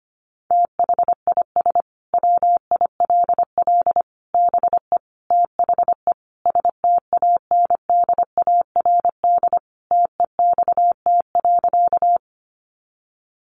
Morse code, heard as a keyed tone, about 25 wpm, 700 Hz. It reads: THIS WILL BE THE STANDARD TEXT.
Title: Sentences from Top 1200 Words 25wpm